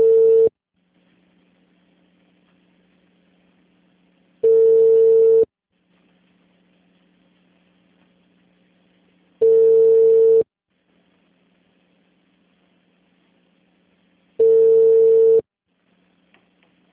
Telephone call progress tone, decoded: ringback tone

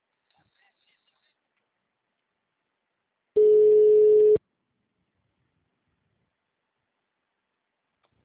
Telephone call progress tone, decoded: ringback tone

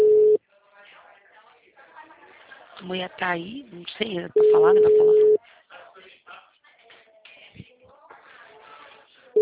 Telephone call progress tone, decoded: ringback tone

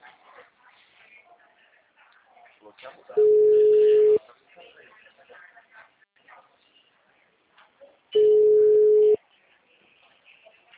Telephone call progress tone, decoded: ringback tone

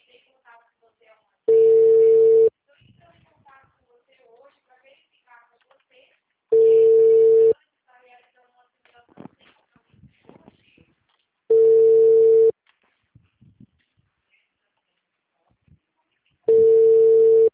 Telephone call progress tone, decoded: ringback tone

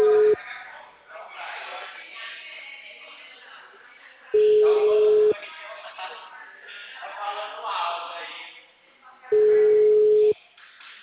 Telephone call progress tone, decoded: ringback tone

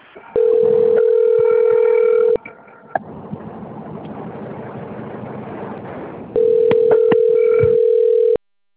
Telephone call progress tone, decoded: ringback tone

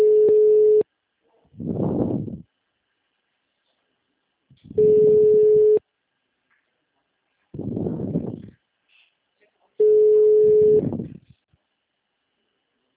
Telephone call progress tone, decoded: ringback tone

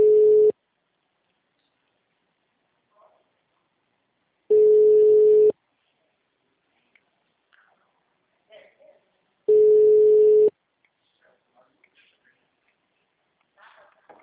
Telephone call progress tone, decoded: ringback tone

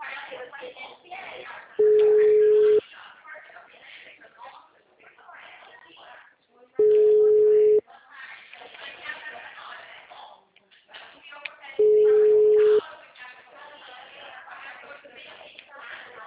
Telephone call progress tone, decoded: ringback tone

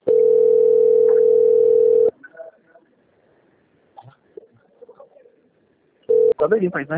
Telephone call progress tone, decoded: ringback tone

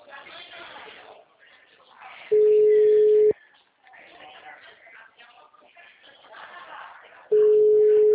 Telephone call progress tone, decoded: ringback tone

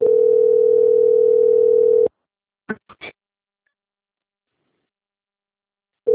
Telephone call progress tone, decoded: ringback tone

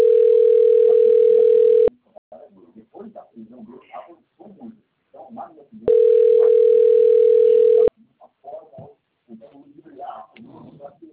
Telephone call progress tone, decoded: ringback tone

